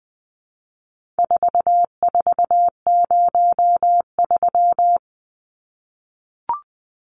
Morse code, heard as a keyed tone, about 20 wpm, 700 Hz. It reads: 4403
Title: Numbers - 4-Digit 20wpm